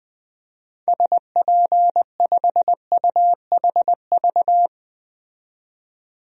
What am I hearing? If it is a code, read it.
Morse code, heard as a keyed tone, about 20 wpm, 700 Hz. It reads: SP5UHV
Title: DXCC-100 Call Signs - 6 Characters - 20wpm